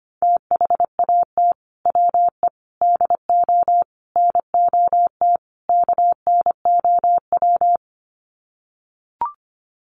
Morse code, heard as a keyed tone, about 25 wpm, 700 Hz. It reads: THAT WE DO NOT KNOW
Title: Sets of 5 Words Encore2 25wpm